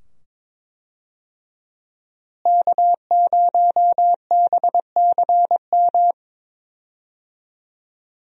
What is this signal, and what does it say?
Morse code, heard as a keyed tone, about 22 wpm, 700 Hz. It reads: K0BCM
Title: USA Call Signs - 5 Characters - Repeated 2x - 22wpm